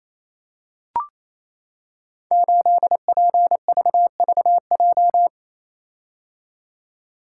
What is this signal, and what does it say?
Morse code, heard as a keyed tone, about 27 wpm, 700 Hz. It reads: 8PVVJ
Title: Code Groups - 5 Letter:Numbers - 28wpm